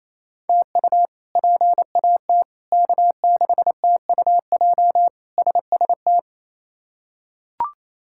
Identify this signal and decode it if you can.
Morse code, heard as a keyed tone, about 28 wpm, 700 Hz. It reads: TU PAT K6TUJ SST
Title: K1USN SST QSO Practice - 28wpm